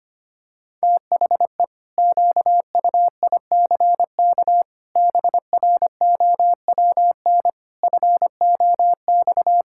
Morse code, heard as a keyed tone, about 25 wpm, 700 Hz. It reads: THE QUICK BROWN FOX
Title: Pangram Challenge v2 - 25wpm